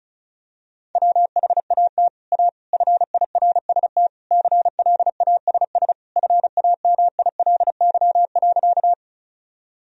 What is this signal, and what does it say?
Morse code, heard as a keyed tone, about 36 wpm, 700 Hz. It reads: WHAT A FIRST CLASS FAMILY.
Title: Sentences from Top 1000 Words - Review Part 2 - 35wpm